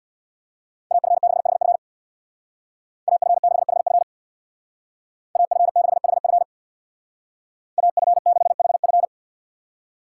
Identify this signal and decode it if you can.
Morse code, heard as a keyed tone, about 64 wpm, 700 Hz. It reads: AF6HF AF6HF AF6HF AF6HF